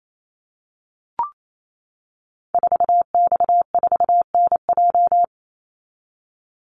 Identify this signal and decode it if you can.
Morse code, heard as a keyed tone, about 28 wpm, 700 Hz. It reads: 4X4NJ
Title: Call Signs - 28wpm